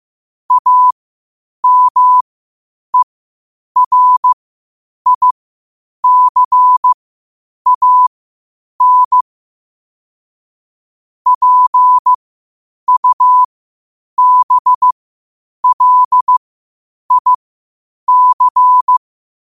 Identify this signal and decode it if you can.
Morse code, heard as a keyed tone, about 15 wpm, spaced as for 8 wpm, 1000 Hz. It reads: AMERICAN PUBLIC
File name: qotdfiles / 8wpm